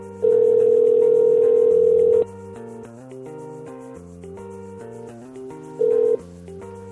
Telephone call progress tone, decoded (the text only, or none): ringback tone